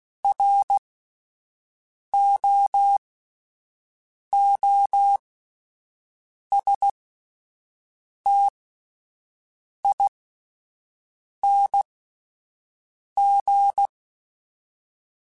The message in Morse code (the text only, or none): ROOSTING